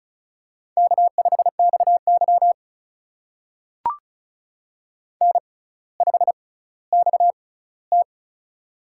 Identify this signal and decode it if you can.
Morse code, heard as a keyed tone, about 36 wpm, spaced as for 18 wpm, 700 Hz. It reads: K5XY N5XT
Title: USA Call Signs - 4 Characters - 6x Character Spacing - 35wpm